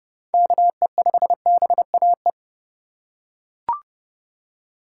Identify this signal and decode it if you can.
Morse code, heard as a keyed tone, about 30 wpm, 700 Hz. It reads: KE5BAE
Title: USA Call Signs - 3x Character Spacing - 30wpm